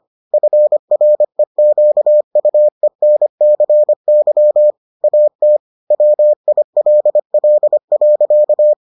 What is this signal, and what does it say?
Morse code, heard as a keyed tone, about 25 wpm, 600 Hz. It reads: FREQUENCY AT WILL.